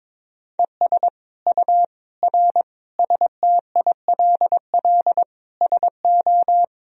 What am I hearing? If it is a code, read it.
Morse code, heard as a keyed tone, about 21 wpm, 700 Hz. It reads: ES U R STILL SO
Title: Sentences from Top 400 Words and Abbreviations 22wpm